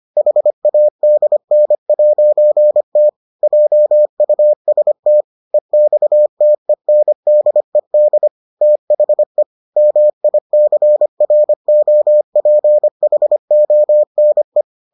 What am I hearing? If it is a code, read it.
Morse code, heard as a keyed tone, about 25 wpm, 600 Hz. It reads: HADN'T JUST EXTENDED THE MICROPHONE